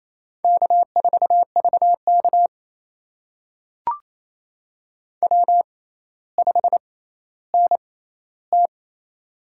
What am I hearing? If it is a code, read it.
Morse code, heard as a keyed tone, about 28 wpm, spaced as for 14 wpm, 700 Hz. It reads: K4VK W5NT
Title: USA Call Signs - 4 Characters - 6x Character Spacing - 28wpm